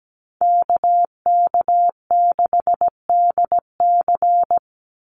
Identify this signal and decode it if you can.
Morse code, heard as a keyed tone, about 17 wpm, 700 Hz. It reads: KK6DC
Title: USA Call Signs 17wpm